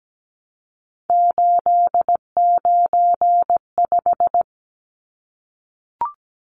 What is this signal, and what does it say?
Morse code, heard as a keyed tone, about 17 wpm, 700 Hz. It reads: 895
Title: Numbers - 3-Digit 17wpm